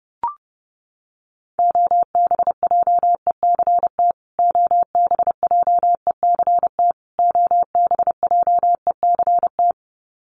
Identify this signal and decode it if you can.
Morse code, heard as a keyed tone, about 30 wpm, 700 Hz. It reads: OBJECT OBJECT OBJECT
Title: Top 500 Words Repeated 30wpm